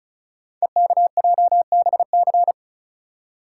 Morse code, heard as a keyed tone, about 36 wpm, 690 Hz. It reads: EKJBC
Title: Code Groups - 5 Letters - 35wpm